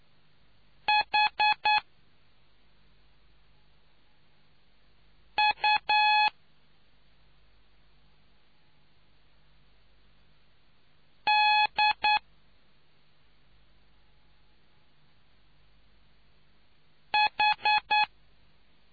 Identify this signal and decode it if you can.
Morse code, heard as a keyed tone, about 9 wpm, 840 Hz. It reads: H U D H